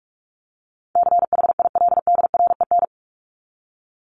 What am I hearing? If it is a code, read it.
Morse code, heard as a keyed tone, about 44 wpm, 700 Hz. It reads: CHILDREN
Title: Top 500 Words 45wpm